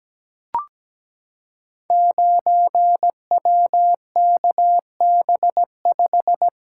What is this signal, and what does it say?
Morse code, heard as a keyed tone, about 17 wpm, 700 Hz. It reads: 9WKB5
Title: Code Groups - 5 Letter:Numbers - 17wpm